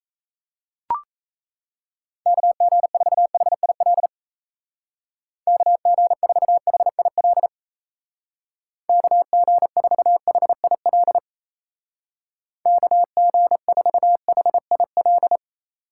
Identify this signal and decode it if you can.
Morse code, heard as a keyed tone, about 34 wpm, 700 Hz. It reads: KG4HIL KG4HIL KG4HIL KG4HIL
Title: Speed Racing - USA Call Signs 28wpm